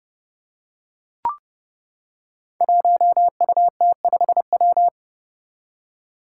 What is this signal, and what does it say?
Morse code, heard as a keyed tone, about 30 wpm, 700 Hz. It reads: 1UT5W
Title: Five Challenging Letter-Number Combinations 30wpm